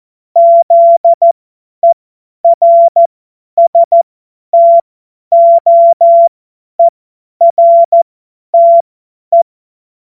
Morse code, heard as a keyed tone, about 14 wpm, spaced as for 10 wpm, 680 Hz. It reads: ZERSTOERTE
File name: Morsekurs / Witze / Witz 09. November 2021, 50 F70